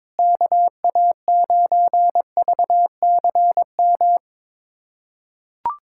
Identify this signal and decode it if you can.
Morse code, heard as a keyed tone, about 22 wpm, 700 Hz. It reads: KA9VCM
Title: USA Callsign DE Callsign 22wpm